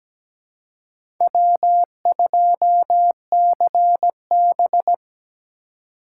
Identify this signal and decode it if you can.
Morse code, heard as a keyed tone, about 17 wpm, 700 Hz. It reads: W2CB
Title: USA Call Signs - 3x Character Spacing - 17wpm